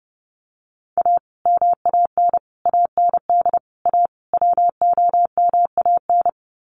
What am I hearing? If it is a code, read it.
Morse code, heard as a keyed tone, about 30 wpm, 700 Hz. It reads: A MAN AND A WOMAN
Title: Sets of 5 Words 30wpm